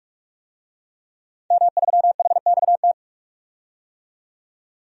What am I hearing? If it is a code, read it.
Morse code, heard as a keyed tone, about 44 wpm, 700 Hz. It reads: M3HXT
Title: DXCC-100 Call Signs - 5 Characters - 45wpm